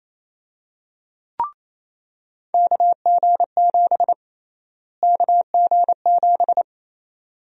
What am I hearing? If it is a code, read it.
Morse code, heard as a keyed tone, about 28 wpm, 700 Hz. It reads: KG7 KG7
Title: USA Call Sign Prefixes - Repeated 2x - 28wpm